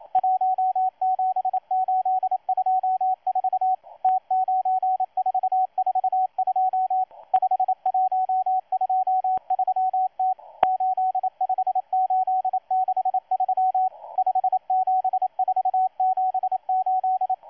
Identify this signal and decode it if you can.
Morse code, heard as a keyed tone, about 27 wpm, 750 Hz, with radio noise. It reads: 17824 T9442 5123T 85863 57478
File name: M12-2016-07-07-1310utc-13926khz-error-restart